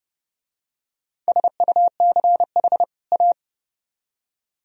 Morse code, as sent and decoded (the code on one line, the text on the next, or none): ... ..- -.-. .... / .-
SUCH A